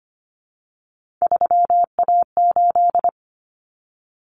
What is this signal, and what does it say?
Morse code, heard as a keyed tone, about 25 wpm, 700 Hz. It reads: 3A8
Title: Common Cut Numbers - 3-Digit 25wpm